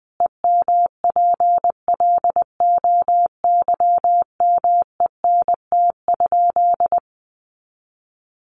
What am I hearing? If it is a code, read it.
Morse code, heard as a keyed tone, about 20 wpm, 700 Hz. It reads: EMPLOYMENT?
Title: Sentences from Top 1500 Words 20wpm